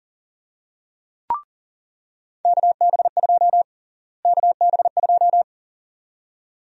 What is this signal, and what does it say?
Morse code, heard as a keyed tone, about 40 wpm, 700 Hz. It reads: KB2 KB2
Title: USA Call Sign Prefixes - Repeated 2x - 40wpm